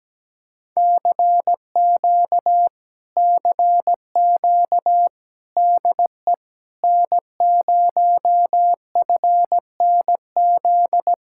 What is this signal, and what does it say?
Morse code, heard as a keyed tone, about 17 wpm, 700 Hz. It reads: CQ CQ DE N0FNZ
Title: Speed-Racing Common QSO Phrases - 17wpm